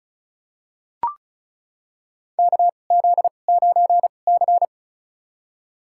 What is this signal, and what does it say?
Morse code, heard as a keyed tone, about 34 wpm, spaced as for 25 wpm, 700 Hz. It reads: KZ9C
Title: USA Call Signs - 4 Characters - 2x Character Spacing - 35wpm